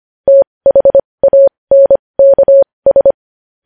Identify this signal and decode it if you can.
Morse code, heard as a keyed tone, about 25 wpm, spaced as for 20 wpm, 550 Hz. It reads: THANKS
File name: podcast / News headlines-1x-25x20